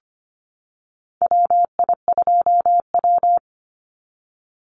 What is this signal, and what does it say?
Morse code, heard as a keyed tone, about 25 wpm, 700 Hz. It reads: WI2W
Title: NA SOTA Chaser Callsigns 25wpm